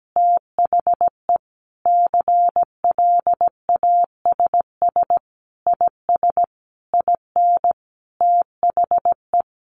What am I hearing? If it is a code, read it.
Morse code, heard as a keyed tone, about 17 wpm, 700 Hz. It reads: THE CLASS IS IN THE
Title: Additional Easy Phrases 17wpm